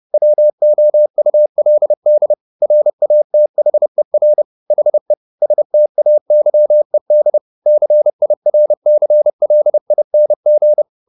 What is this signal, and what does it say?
Morse code, heard as a keyed tone, about 30 wpm, 600 Hz. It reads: WOULD RATHER HE STAYED CIRCLING